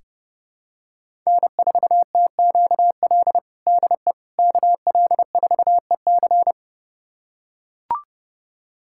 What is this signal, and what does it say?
Morse code, heard as a keyed tone, about 32 wpm, 700 Hz. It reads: N4TQL DE KL4EC
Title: USA Callsign DE Callsign - 5 Characters - 30wpm